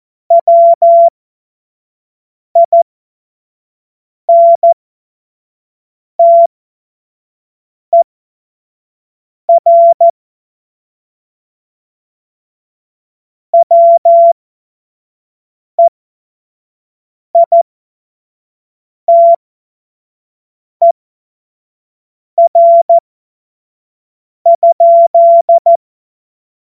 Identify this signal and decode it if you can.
Morse code, heard as a keyed tone, about 14 wpm, spaced as for 5 wpm, 680 Hz. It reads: WINTER WEITER?
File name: Morsekurs / Witze / Witz 08. August 2023 25 F70